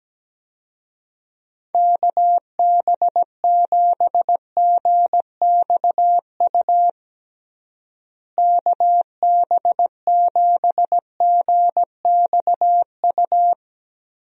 Morse code, heard as a keyed tone, about 17 wpm, 700 Hz. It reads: KB7GXU KB7GXU
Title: USA Call Signs - 6 Characters - Repeated 3x - 17wpm